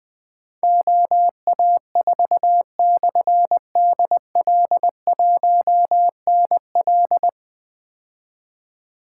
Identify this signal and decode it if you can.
Morse code, heard as a keyed tone, about 20 wpm, 700 Hz. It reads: OA4/DL1NL
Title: Difficult Call Signs 20wpm